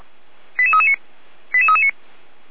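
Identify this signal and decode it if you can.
Five-tone selective call (ZVEI), heard as quick runs of five tones, tones of about 80 ms. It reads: 88339 88339
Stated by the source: ZVEI five-tone call 88341, received by SDR for Feuerwehr Orlamünde.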